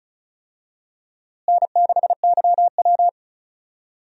Morse code, heard as a keyed tone, about 36 wpm, 700 Hz. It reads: N6YW
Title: USA Call Signs - 4 Characters - Repeated 2x - 35wpm